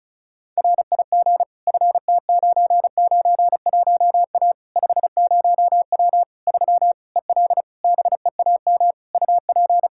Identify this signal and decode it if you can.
Morse code, heard as a keyed tone, about 36 wpm, 700 Hz. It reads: RIG FT991A 50W 3 EL BEAM UP
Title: Practice QSO - Exchange 2 - Info - 35wpm